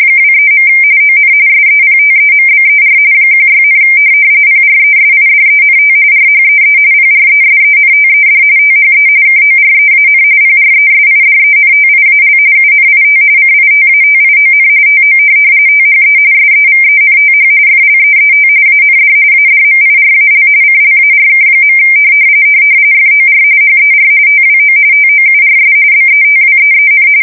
RTTY, about 45.45 baud. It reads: 23. ON SEPTEMBER 16, PRESIDENT RHEE PRESENTED THE KOREAN PRESIDENTIAL UNIT CITATION TO THE SEVENTH FLEET AND ADMIRAL CLARK. FOR HER TWO TOURS BETWEEN
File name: fsk170 / USS-New-Jersey